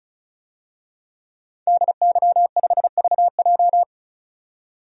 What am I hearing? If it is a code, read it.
Morse code, heard as a keyed tone, about 36 wpm, 700 Hz. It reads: DY5VJ
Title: DXCC-100 Call Signs - 5 Characters - 35wpm